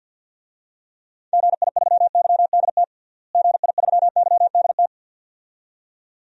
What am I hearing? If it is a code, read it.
Morse code, heard as a keyed tone, about 50 wpm, 700 Hz. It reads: GI3YDT GI3YDT
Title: DXCC-100 Call Signs - 6 Characters - Repeated 2x - 50wpm